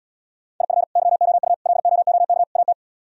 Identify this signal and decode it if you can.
Morse code, heard as a keyed tone, about 77 wpm, 700 Hz. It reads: IF YOU LOOK AT